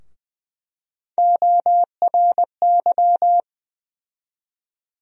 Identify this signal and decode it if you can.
Morse code, heard as a keyed tone, about 20 wpm, 700 Hz. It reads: ORY
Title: USA Call Sign Postfixes - 3 Letters 20wpm